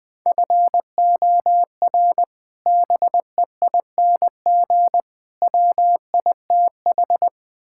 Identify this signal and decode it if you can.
Morse code, heard as a keyed tone, about 20 wpm, 700 Hz. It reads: FOR BEING WITH
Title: Build A Phrase - 4-Word Phrases - 20wpm